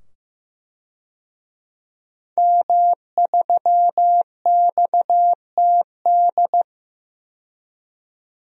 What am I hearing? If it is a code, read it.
Morse code, heard as a keyed tone, about 15 wpm, 700 Hz. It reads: M3XTD
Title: DXCC-100 Call Signs - 5 Characters - 15wpm